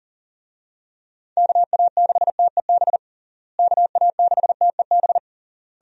Morse code, heard as a keyed tone, about 37 wpm, 700 Hz. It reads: KA6TEB KA6TEB
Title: USA Call Signs - 6 Characters - Repeated 3x - 40wpm